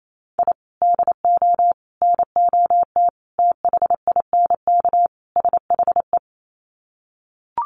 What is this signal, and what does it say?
Morse code, heard as a keyed tone, about 28 wpm, 700 Hz. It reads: I DO NOT THINK SHE
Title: Sets of 5 Words 28wpm